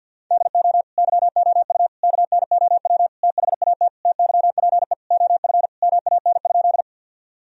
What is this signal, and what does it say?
Morse code, heard as a keyed tone, about 50 wpm, 700 Hz. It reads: DO YOU KNOW THAT TYPE OF MAN?
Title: Sentences from Top 1000 Words - Review Part 3 - 50wpm